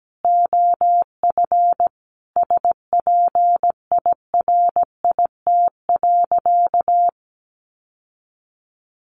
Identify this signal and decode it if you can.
Morse code, heard as a keyed tone, about 17 wpm, 700 Hz. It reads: OF SPIRIT.
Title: Sentences from Top 1400 Words 17wpm